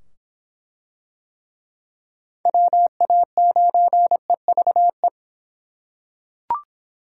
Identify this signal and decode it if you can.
Morse code, heard as a keyed tone, about 24 wpm, 700 Hz. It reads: WA9EVE